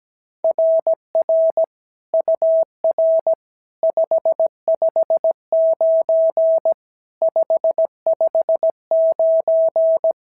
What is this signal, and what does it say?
Morse code, heard as a keyed tone, about 16 wpm, 650 Hz. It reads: RR UR 559 559
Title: SOTA QSO Practice 17wpm